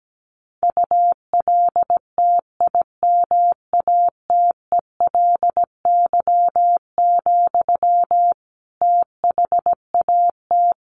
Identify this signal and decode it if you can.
Morse code, heard as a keyed tone, about 17 wpm, 700 Hz. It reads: ULTIMATELY, THAT
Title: Sentences from Top 1200 Words 17wpm